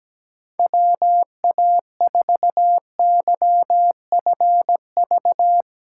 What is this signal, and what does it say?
Morse code, heard as a keyed tone, about 17 wpm, 700 Hz. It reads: WA4YFV
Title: USA Call Signs - 6 Characters - 6x Character Spacing - 17wpm